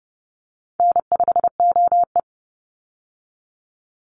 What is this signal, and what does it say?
Morse code, heard as a keyed tone, about 30 wpm, 700 Hz. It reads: N5OE
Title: Easy Call Signs Repeated 2x - 30wpm